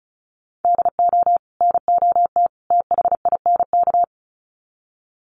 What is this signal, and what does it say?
Morse code, heard as a keyed tone, about 34 wpm, 700 Hz. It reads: DO NOT THINK
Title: Sets of 3 Words 35wpm